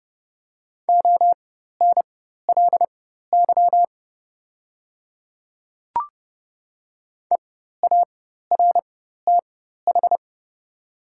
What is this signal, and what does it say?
Morse code, heard as a keyed tone, about 30 wpm, spaced as for 14 wpm, 700 Hz. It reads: ONLY EARTH
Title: 4x Character Spacing - Top 200 Words - 30wpm